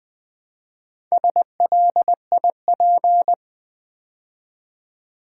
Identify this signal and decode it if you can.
Morse code, heard as a keyed tone, about 20 wpm, 700 Hz. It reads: SLIP
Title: Build A Word - 2 Letters At A Time - 4-Letter Words - 20wpm